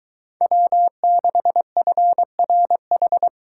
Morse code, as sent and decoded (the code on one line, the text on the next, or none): .-- -.... ..-. .-. ....
W6FRH